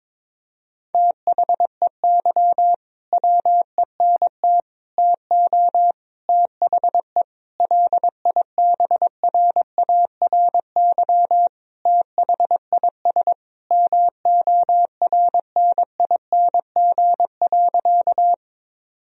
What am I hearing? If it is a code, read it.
Morse code, heard as a keyed tone, about 22 wpm, 700 Hz. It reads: THEY WENT TO THE LIBRARY THIS MORNING.
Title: Sentences from Top 1000 Words - Review Part 4 - 22wpm